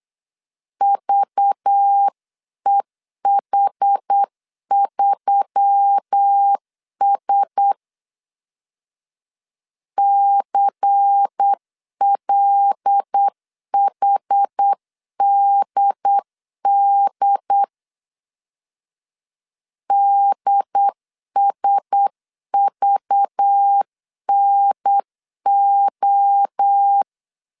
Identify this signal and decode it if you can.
Morse code, heard as a keyed tone, about 9 wpm, 800 Hz. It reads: VEH3S CLHDD DSVNO
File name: CW / DET / 1 6 1500C S C 8wpm 5219